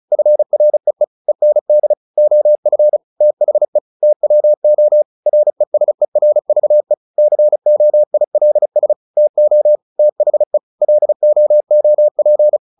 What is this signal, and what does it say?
Morse code, heard as a keyed tone, about 34 wpm, 600 Hz. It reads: FREE END OF THE TWO RESERVE COILS TO THE LOOP
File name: the old man and the sea / the old man and the sea 1 35wpm